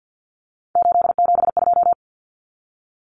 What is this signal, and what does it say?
Morse code, heard as a keyed tone, about 48 wpm, 700 Hz. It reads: 872
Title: Common Cut Numbers - 3-Digit 50wpm